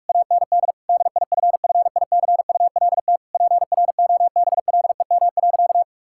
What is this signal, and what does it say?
Morse code, heard as a keyed tone, about 46 wpm, 700 Hz. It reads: AND DIFFICULT PROBLEM.
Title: Sentences from Top 1000 Words - Review Part 6 - 45wpm